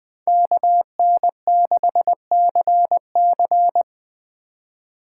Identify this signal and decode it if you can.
Morse code, heard as a keyed tone, about 21 wpm, 700 Hz. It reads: KN6CC
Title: USA Call Signs - 5 Characters - 2x Character Spacing - 20wpm